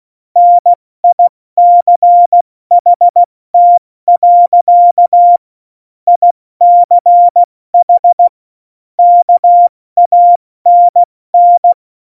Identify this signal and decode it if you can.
Morse code, heard as a keyed tone, about 16 wpm, 700 Hz. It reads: NICHT. ICH KANN
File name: Morsekurs / Witze / Witz 27. November 2018, 70 F80